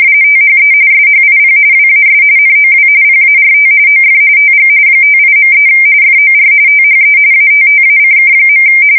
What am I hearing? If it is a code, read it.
RTTY, about 45.45 baud. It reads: THE SHIP WAS NOW PART OF TG 58.3 FOR THE ATTACK.
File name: fsk170 / USS-Massachusetts